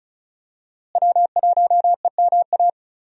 Morse code, heard as a keyed tone, about 36 wpm, 700 Hz. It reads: W1EMA
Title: USA Call Signs - 5 Characters - Repeated 3x - 35wpm